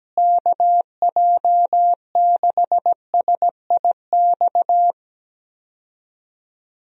Morse code, heard as a keyed tone, about 17 wpm, 700 Hz. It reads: KJ6SIX
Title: USA Call Signs - 6 Characters - Repeated 2x - 17wpm